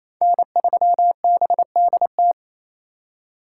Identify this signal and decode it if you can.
Morse code, heard as a keyed tone, about 28 wpm, 700 Hz. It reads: N3BDT
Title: USA Call Signs - 5 Characters - 2x Character Spacing - 28wpm